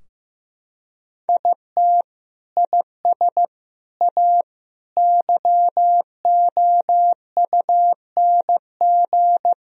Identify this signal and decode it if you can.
Morse code, heard as a keyed tone, about 15 wpm, 700 Hz. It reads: IT IS A YOUNG
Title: Sentences from Top 1000 Words - Review Part 4 - 15wpm